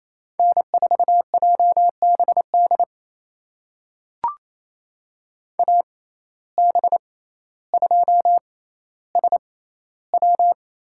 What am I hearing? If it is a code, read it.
Morse code, heard as a keyed tone, about 28 wpm, spaced as for 14 wpm, 700 Hz. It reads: N4JBD AB2SW
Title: USA Call Signs - 5 Characters - 6x Character Spacing - 28wpm